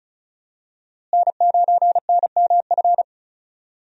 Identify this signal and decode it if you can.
Morse code, heard as a keyed tone, about 34 wpm, 700 Hz. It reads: N9NMF
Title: USA Call Signs - 5 Characters - Repeated 2x - 35wpm